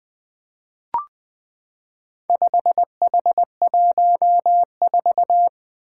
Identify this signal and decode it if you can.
Morse code, heard as a keyed tone, about 21 wpm, 700 Hz. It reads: 5H14
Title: Four Letter-Number Combinations - 20wpm